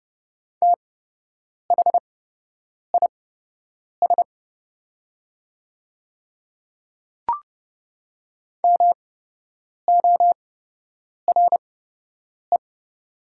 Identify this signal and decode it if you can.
Morse code, heard as a keyed tone, about 30 wpm, spaced as for 8 wpm, 700 Hz. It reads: THIS MORE
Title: 8x Character Spacing - Top 100 Words - 30wpm